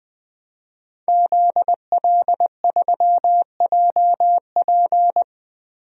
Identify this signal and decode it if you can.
Morse code, heard as a keyed tone, about 20 wpm, 700 Hz. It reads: ZL3JP
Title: DXCC-100 Call Signs - 5 Characters - Repeated 2x - 20wpm